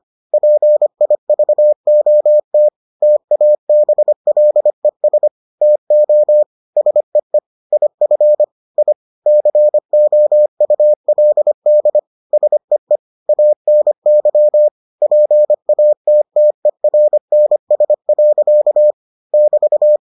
Morse code, heard as a keyed tone, about 25 wpm, 600 Hz. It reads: PIVOT TABLES TO SEE IF I COULD SEE ANY PATTERNS. <BT>